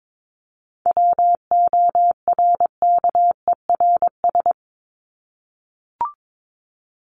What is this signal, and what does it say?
Morse code, heard as a keyed tone, about 22 wpm, 700 Hz. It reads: WORKERS
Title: ES:S Suffix - Top 100 Words with Base Words 22wpm